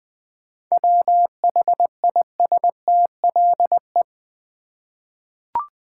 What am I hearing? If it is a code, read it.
Morse code, heard as a keyed tone, about 20 wpm, 700 Hz. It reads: WHISTLE
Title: Easy Words with Hints - 20wpm